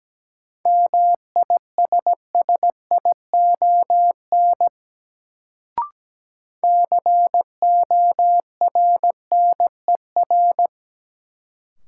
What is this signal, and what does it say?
Morse code, heard as a keyed tone, about 17 wpm, 700 Hz. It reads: MISSION CORNER
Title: Sentences from Top 1600 Words - New Word Drill Alt - 17wpm